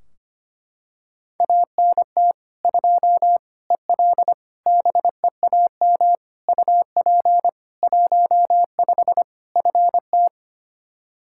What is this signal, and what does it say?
Morse code, heard as a keyed tone, about 25 wpm, 700 Hz. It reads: ANT 2 EL BEAM UP 15 FT
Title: Practice QSO - Exchange 2 - Info - 25wpm